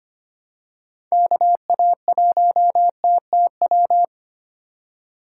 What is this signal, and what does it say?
Morse code, heard as a keyed tone, about 25 wpm, 700 Hz. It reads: KA1TTW
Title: USA Call Signs - 6 Characters - 2x Character Spacing - 25wpm